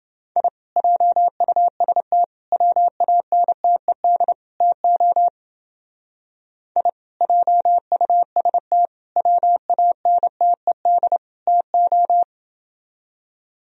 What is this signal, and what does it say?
Morse code, heard as a keyed tone, about 29 wpm, 700 Hz. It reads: I JUST WANTED TO I JUST WANTED TO